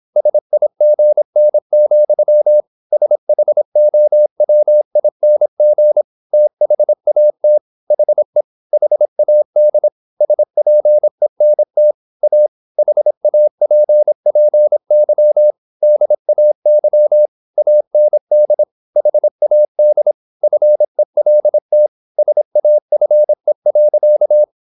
Morse code, heard as a keyed tone, about 26 wpm, 600 Hz. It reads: SIGN, SHOWING THAT HE HAD SPENT A HAPPY DAY AND HAD FELT SAFE.
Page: Never 26wpm | CW med Gnister